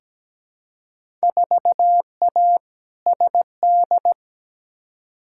Morse code, heard as a keyed tone, about 17 wpm, 700 Hz. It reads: 4A SD
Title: Field Day Exchanges - 17wpm